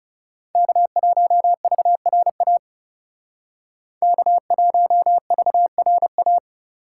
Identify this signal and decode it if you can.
Morse code, heard as a keyed tone, about 40 wpm, 700 Hz. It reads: K1VRA K1VRA